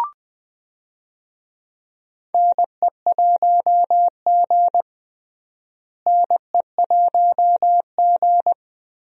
Morse code, heard as a keyed tone, about 20 wpm, 700 Hz. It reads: NE1G NE1G
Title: USA Call Signs - 4 Characters - Repeated 2x - 20wpm